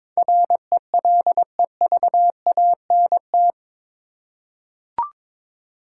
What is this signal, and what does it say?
Morse code, heard as a keyed tone, about 21 wpm, 700 Hz. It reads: RELEVANT
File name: practice / 22wpm / Eight Letter Words - 22wpm